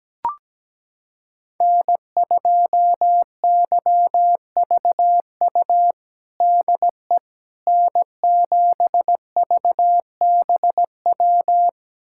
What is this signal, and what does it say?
Morse code, heard as a keyed tone, about 17 wpm, 700 Hz. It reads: N2YVU DE N7VBW
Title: USA Callsign DE Callsign 17wpm